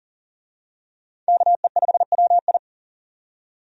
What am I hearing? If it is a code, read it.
Morse code, heard as a keyed tone, about 40 wpm, 700 Hz. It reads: KE5WI
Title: USA Call Signs - 5 Characters - 40wpm